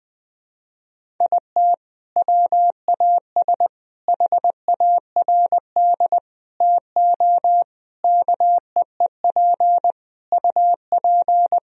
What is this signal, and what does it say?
Morse code, heard as a keyed tone, about 21 wpm, 700 Hz. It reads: IT WAS HARD TO KEEP UP
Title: Sentences from Top 1000 Words - Review Part 1 - 20wpm